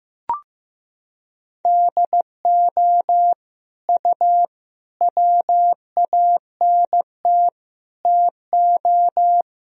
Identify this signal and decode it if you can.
Morse code, heard as a keyed tone, about 15 wpm, 700 Hz. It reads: DO U WANT TO
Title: Sentences from Top 300 Words and Abbreviations 15wpm